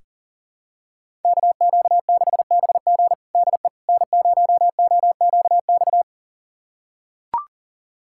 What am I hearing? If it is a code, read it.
Morse code, heard as a keyed tone, about 40 wpm, 700 Hz. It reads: KQ6BG DE N0OQX
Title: USA Callsign DE Callsign - 5 Characters - 40wpm